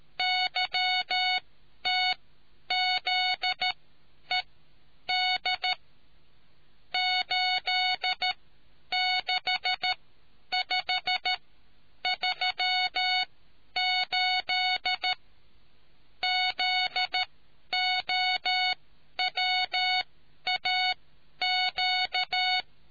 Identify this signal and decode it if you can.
Morse code, heard as a keyed tone, about 13 wpm, spaced as for 9 wpm, 740 Hz. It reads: YTZED 86538 ZOWAQ